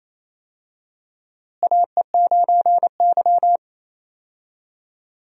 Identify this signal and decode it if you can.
Morse code, heard as a keyed tone, about 28 wpm, 700 Hz. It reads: AE9Y
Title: USA Call Signs - 4 Characters - Repeated 2x - 28wpm